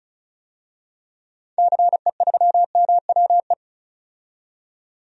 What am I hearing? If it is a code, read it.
Morse code, heard as a keyed tone, about 34 wpm, 700 Hz. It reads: CE3MWE
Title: DXCC-100 Call Signs - 6 Characters - 35wpm